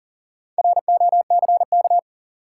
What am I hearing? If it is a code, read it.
Morse code, heard as a keyed tone, about 40 wpm, 700 Hz. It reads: ROCK
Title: Sentences from Top 400 Words - New Word Drill - 40wpm